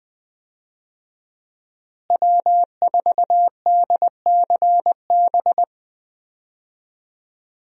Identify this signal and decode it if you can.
Morse code, heard as a keyed tone, about 20 wpm, 700 Hz. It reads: W4DCB
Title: USA Call Signs - 5 Characters - Repeated 3x - 20wpm